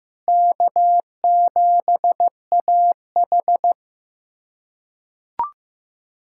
Morse code, heard as a keyed tone, about 15 wpm, 700 Hz. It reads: K7AH
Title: USA Callsign DE Callsign - 4 Characters - 15wpm